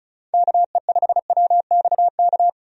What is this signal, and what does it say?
Morse code, heard as a keyed tone, about 35 wpm, 700 Hz. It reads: KE5WXK
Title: USA Call Signs - 6 Characters - 35wpm